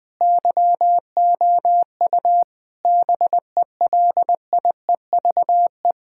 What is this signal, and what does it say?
Morse code, heard as a keyed tone, about 21 wpm, 700 Hz. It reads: YOU BELIEVE
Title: Sentences from Top 1600 Words 20wpm